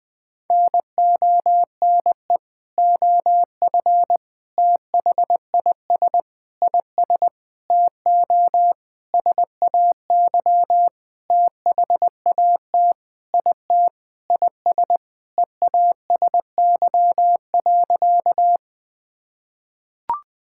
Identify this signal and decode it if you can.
Morse code, heard as a keyed tone, about 20 wpm, 700 Hz. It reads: NONE OF THIS IS TO SAY THAT IT IS EASY.
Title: Sentences with 7-Word Phrases - Spoken Beginning - 20wpm